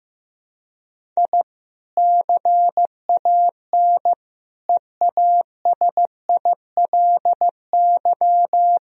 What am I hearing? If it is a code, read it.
Morse code, heard as a keyed tone, about 15 wpm, 700 Hz. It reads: I CAN EASILY
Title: Sentences from Top 1000 Words - Review Part 1 - 15wpm